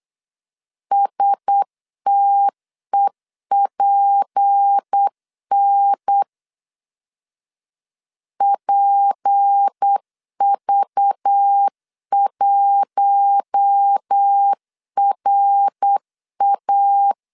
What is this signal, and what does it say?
Morse code, heard as a keyed tone, about 8 wpm, 800 Hz. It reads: STEPN PV1RA
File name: LEZIONI / 6 PKX0 / 6 P 8wpm 3025